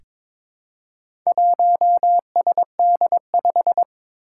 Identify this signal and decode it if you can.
Morse code, heard as a keyed tone, about 22 wpm, 700 Hz. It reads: 1SD5
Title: Four Challenging Letter-Number Combinations 22wpm